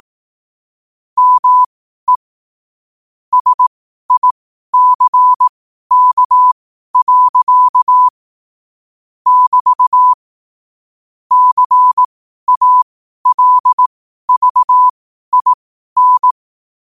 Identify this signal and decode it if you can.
Morse code, heard as a keyed tone, about 18 wpm, spaced as for 12 wpm, 1000 Hz. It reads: ME SICK. <BT> CALVIN
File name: qotdfiles / 12wpm